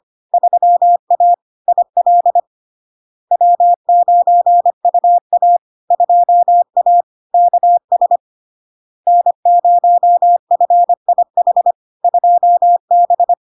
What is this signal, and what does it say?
Morse code, heard as a keyed tone, about 25 wpm, 700 Hz. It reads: 3A IL W9UA 2A KS N0FIH 2B
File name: Log-in-CW-25wpm-700-Hz